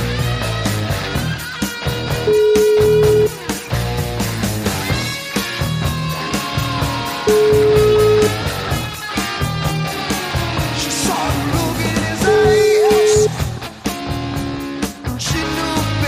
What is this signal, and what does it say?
Ringback tone (Europe, 425 Hz), with noise.